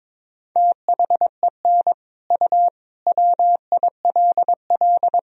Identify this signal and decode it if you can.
Morse code, heard as a keyed tone, about 22 wpm, 700 Hz. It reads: THEN U WILL
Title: Sentences from Top 400 Words and Abbreviations 22wpm